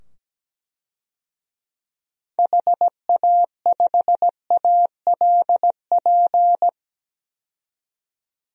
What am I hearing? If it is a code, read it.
Morse code, heard as a keyed tone, about 17 wpm, 700 Hz. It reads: HA5ALP
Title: DXCC-100 Call Signs - 6 Characters - 17wpm